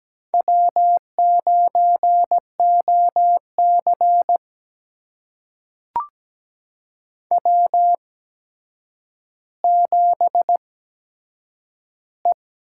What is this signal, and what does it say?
Morse code, heard as a keyed tone, about 17 wpm, spaced as for 8 wpm, 700 Hz. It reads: W9OC W7E
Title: USA Call Signs - 4 Characters - 8x Character Spacing - 17wpm